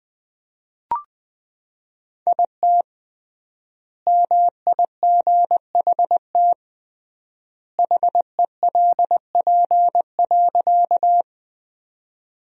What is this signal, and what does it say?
Morse code, heard as a keyed tone, about 21 wpm, 700 Hz. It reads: IT MIGHT HELP.
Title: 3x Word-Spacing - Sentences from Top 300 Words - 20wpm